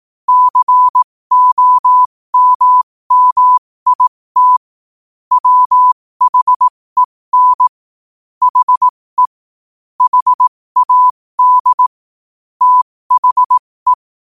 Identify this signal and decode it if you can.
Morse code, heard as a keyed tone, about 18 wpm, spaced as for 15 wpm, 1000 Hz. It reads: COMMIT WHEN HE HAD THE
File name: qotdfiles / 15wpm